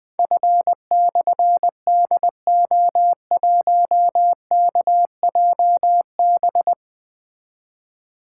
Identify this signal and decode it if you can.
Morse code, heard as a keyed tone, about 20 wpm, 700 Hz. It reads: F/DO1KJB
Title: Difficult Call Signs 20wpm